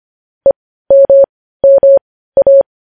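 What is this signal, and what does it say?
Morse code, heard as a keyed tone, about 25 wpm, spaced as for 15 wpm, 550 Hz. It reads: EMMA
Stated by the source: Morse practice of news headlines at 15 wpm.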